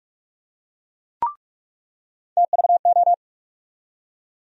Morse code, heard as a keyed tone, about 45 wpm, 700 Hz. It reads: TVO
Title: USA Call Sign Postfixes - 3 Letters 45wpm